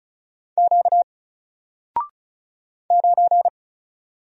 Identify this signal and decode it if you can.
Morse code, heard as a keyed tone, about 36 wpm, 700 Hz. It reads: Q 9
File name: Single-Letter-Number-35wpm